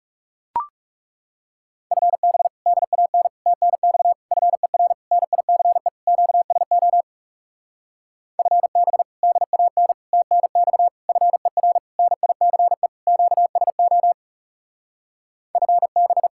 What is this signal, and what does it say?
Morse code, heard as a keyed tone, about 42 wpm, 700 Hz. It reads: FB DAN TNX FER NICE QSO FB DAN TNX FER NICE QSO FB